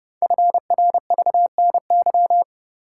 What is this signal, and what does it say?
Morse code, heard as a keyed tone, about 30 wpm, 700 Hz. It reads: FRVNY
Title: Code Groups - 5 Letters - 30wpm